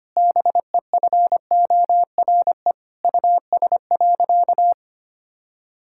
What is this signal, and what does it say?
Morse code, heard as a keyed tone, about 25 wpm, 700 Hz. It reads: BEFORE US.
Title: Sentences from Top 1000 Words - Review Part 5 - 25wpm